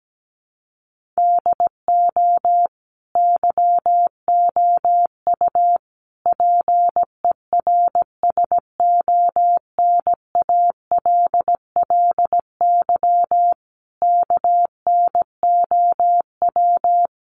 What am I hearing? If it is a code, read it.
Morse code, heard as a keyed tone, about 17 wpm, 700 Hz. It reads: DO YOU PERSONALLY KNOW
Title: Sentences from Top 1400 Words 17wpm